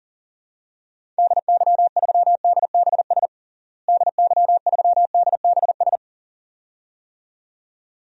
Morse code, heard as a keyed tone, about 40 wpm, 700 Hz. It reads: DY3DBS DY3DBS
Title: DXCC-100 Call Signs - 6 Characters - Repeated 2x - 40wpm